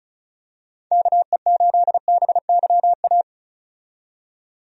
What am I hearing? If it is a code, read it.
Morse code, heard as a keyed tone, about 36 wpm, 700 Hz. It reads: KE8BYA
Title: USA Call Signs - 6 Characters - 35wpm